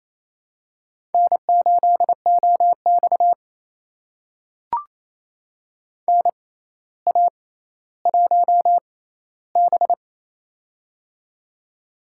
Morse code, heard as a keyed tone, about 28 wpm, spaced as for 14 wpm, 700 Hz. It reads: N8OX NA1B
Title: USA Call Signs - 4 Characters - 6x Character Spacing - 28wpm